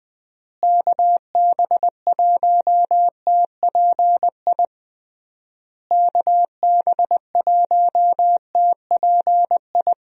Morse code, heard as a keyed tone, about 21 wpm, 700 Hz. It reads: KB1TPI KB1TPI
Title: USA Call Signs - 6 Characters - Repeated 3x - 20wpm